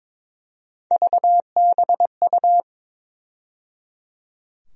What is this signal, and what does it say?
Morse code, heard as a keyed tone, about 23 wpm, 700 Hz. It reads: VBU
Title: USA Call Sign Postfixes - 3 Letters 22wpm